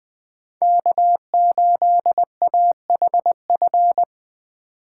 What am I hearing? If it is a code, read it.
Morse code, heard as a keyed tone, about 24 wpm, 700 Hz. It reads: K8AHF
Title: Speed Racing - USA Call Signs 20wpm